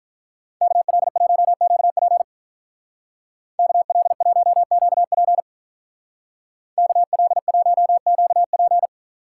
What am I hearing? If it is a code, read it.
Morse code, heard as a keyed tone, about 43 wpm, 700 Hz. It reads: KL1QP KL1QP KL1QP
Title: Speed Racing - USA 2x2 Call Signs 35wpm